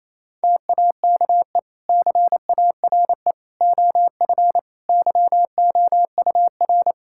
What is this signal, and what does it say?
Morse code, heard as a keyed tone, about 27 wpm, 700 Hz. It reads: TAKE CARE OF YOUR
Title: Sentences from Top 1000 Words - Review Part 5 - 28wpm